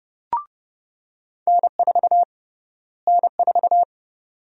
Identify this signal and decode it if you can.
Morse code, heard as a keyed tone, about 31 wpm, 700 Hz. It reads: N4 N4
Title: USA Call Sign Prefixes - Repeated 2x - 30wpm